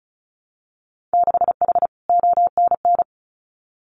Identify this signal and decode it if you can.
Morse code, heard as a keyed tone, about 36 wpm, 700 Hz. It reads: 6H ONN
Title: Winter Field Day Exchanges Encore 35wpm